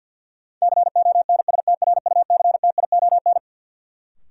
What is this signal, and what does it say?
Morse code, heard as a keyed tone, about 52 wpm, 700 Hz. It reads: CONSTRUCTION
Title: Sentences from Top 1600 Words - New Word Drill Alt - 50wpm